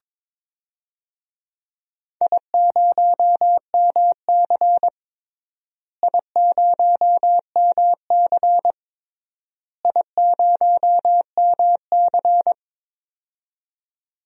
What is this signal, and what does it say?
Morse code, heard as a keyed tone, about 21 wpm, 700 Hz. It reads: I0MC I0MC I0MC
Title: DXCC-100 Call Signs - 4 Characters - Repeated 3x - 22wpm